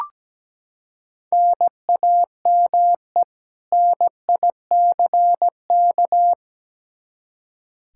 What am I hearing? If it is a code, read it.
Morse code, heard as a keyed tone, about 17 wpm, 700 Hz. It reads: NAME NICK
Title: Practice QSO - Exchange 1 - Beginning - 17wpm